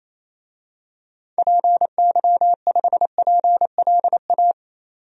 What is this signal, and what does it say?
Morse code, heard as a keyed tone, about 28 wpm, 700 Hz. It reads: PY5PLA
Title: DXCC-100 Call Signs - 6 Characters - 28wpm